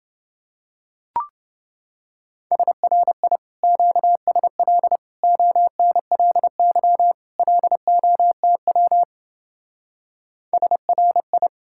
Morse code, heard as a keyed tone, about 29 wpm, 700 Hz. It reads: SRI QSL ONLY LOTW SRI